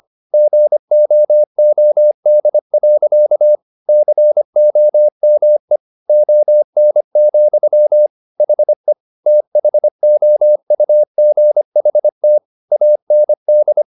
Morse code, heard as a keyed tone, about 25 wpm, 600 Hz. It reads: GOOD. COME ON, HE THOUGHT AND
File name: the old man and the sea part 1 25wpm